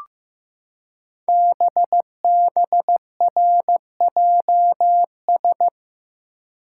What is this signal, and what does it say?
Morse code, heard as a keyed tone, about 15 wpm, 700 Hz. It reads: BBRJS
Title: Code Groups - 5 Letters - 15wpm